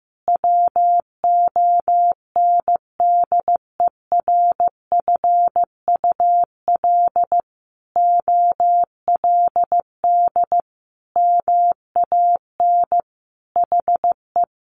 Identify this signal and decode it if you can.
Morse code, heard as a keyed tone, about 15 wpm, 700 Hz. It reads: WONDERFUL OLD MAN HE
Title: Sentences from Top 1100 Words 15wpm